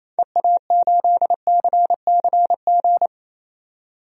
Morse code, heard as a keyed tone, about 28 wpm, 700 Hz. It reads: EA8CCG
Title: DXCC-100 Call Signs - 6 Characters - 28wpm